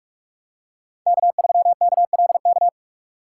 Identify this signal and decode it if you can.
Morse code, heard as a keyed tone, about 44 wpm, 700 Hz. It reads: K3KLK